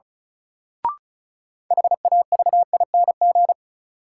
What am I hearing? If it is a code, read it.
Morse code, heard as a keyed tone, about 36 wpm, 700 Hz. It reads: HAVING
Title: Sets of 5 Words Encore - New Word Drill - 35wpm